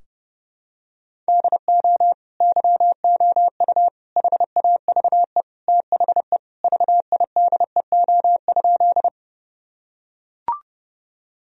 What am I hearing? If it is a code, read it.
Morse code, heard as a keyed tone, about 32 wpm, 700 Hz. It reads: DO YOU HAVE THE VIDEO?
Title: Sentences from Top 1000 Words - Review Part 4 - 30wpm